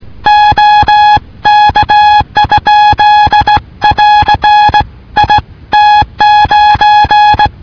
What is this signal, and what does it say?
Morse code, heard as a keyed tone, about 18 wpm, 840 Hz. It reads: OK? <AR> IT9